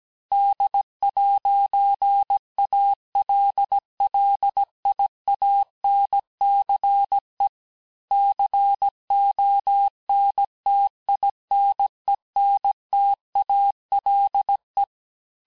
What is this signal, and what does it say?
Morse code, heard as a keyed tone, about 17 wpm, 800 Hz. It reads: D'ALLIANCE CONTINENTALE
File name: cours-du-lundi files / cours-1587-texte-clair-vitesse-16mpm-puis-21mpm